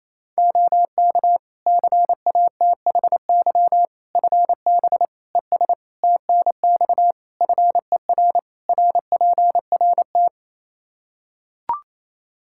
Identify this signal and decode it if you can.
Morse code, heard as a keyed tone, about 28 wpm, 700 Hz. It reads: OK CATHY FB ES TNX FER RPRT
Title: Practice QSO - Exchange 2 - Info - 28wpm